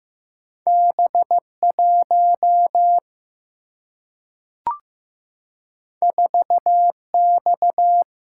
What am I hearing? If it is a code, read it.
Morse code, heard as a keyed tone, about 15 wpm, 700 Hz. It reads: B1 4X
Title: Single-Letter Single-Number 15wpm